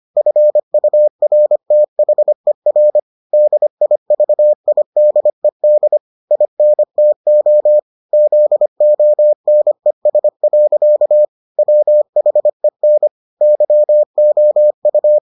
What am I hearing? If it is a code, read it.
Morse code, heard as a keyed tone, about 25 wpm, 600 Hz. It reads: FURTHER DIVIDED INTO ZONES. WHEN YOU